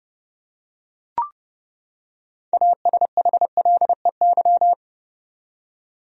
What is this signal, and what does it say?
Morse code, heard as a keyed tone, about 29 wpm, 700 Hz. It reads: ASHLEY
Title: USA Common Names - 30wpm